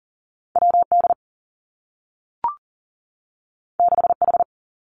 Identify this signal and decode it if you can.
Morse code, heard as a keyed tone, about 42 wpm, 700 Hz. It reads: W D 6 H
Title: Two Challenging Letter-Number Combinations 40wpm